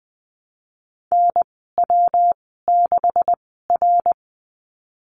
Morse code, heard as a keyed tone, about 20 wpm, spaced as for 15 wpm, 700 Hz. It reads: NW6R
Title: 147 Lesson 24g - Letter y - Call Signs 20wpm